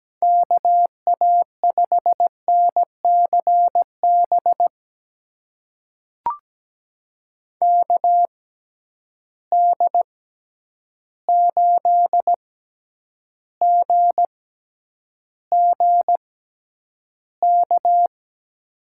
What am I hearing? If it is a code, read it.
Morse code, heard as a keyed tone, about 17 wpm, spaced as for 9 wpm, 700 Hz. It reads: KA5NCB KD8GGK
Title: USA Call Signs - 6 Characters - 6x Character Spacing - 17wpm